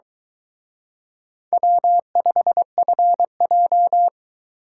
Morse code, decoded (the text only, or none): W5FJ